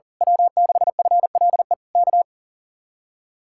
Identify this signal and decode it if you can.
Morse code, heard as a keyed tone, about 41 wpm, 700 Hz. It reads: W6FLE K
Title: Speed-Racing Common QSO Phrases - 40wpm